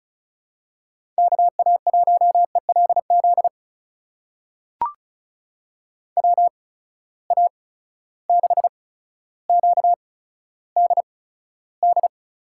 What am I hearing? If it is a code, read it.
Morse code, heard as a keyed tone, about 36 wpm, spaced as for 15 wpm, 700 Hz. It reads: KA1ELZ WA6QDD
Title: USA Call Signs - 6 Characters - 8x Character Spacing - 35wpm